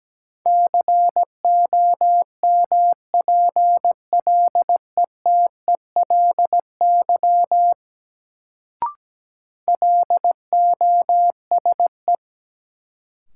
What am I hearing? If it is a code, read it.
Morse code, heard as a keyed tone, about 17 wpm, 700 Hz. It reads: COMPLETELY LOSE
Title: Sentences from Top 1100 Words - New Word Drill Alt - 17wpm